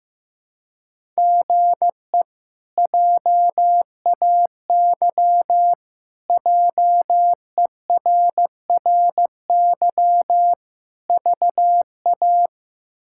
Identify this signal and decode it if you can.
Morse code, heard as a keyed tone, about 15 wpm, 700 Hz. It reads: GE JAY JERRY VA
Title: K1USN SST QSO Practice - 15wpm